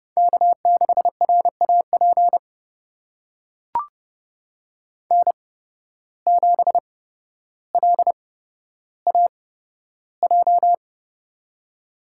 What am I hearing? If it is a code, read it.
Morse code, heard as a keyed tone, about 30 wpm, spaced as for 13 wpm, 700 Hz. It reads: K6RAP N7LAJ
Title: USA Call Signs - 5 Characters - 8x Character Spacing - 30wpm